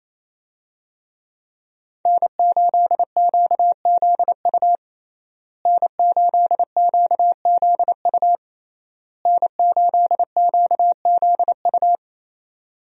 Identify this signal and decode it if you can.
Morse code, heard as a keyed tone, about 28 wpm, 700 Hz. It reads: N8QZU N8QZU N8QZU
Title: USA Call Signs - 5 Characters - Repeated 3x - 28wpm